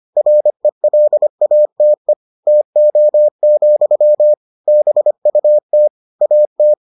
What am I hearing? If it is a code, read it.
Morse code, heard as a keyed tone, about 25 wpm, 600 Hz. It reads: RELATE TO, BUT AT